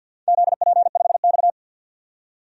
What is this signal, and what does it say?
Morse code, heard as a keyed tone, about 48 wpm, 700 Hz. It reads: ZP5X
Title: DXCC-100 Call Signs - 4 Characters - 50wpm